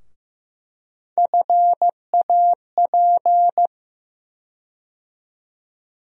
Morse code, heard as a keyed tone, about 15 wpm, 700 Hz. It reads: FAP
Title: USA Call Sign Postfixes - 3 Letters 15wpm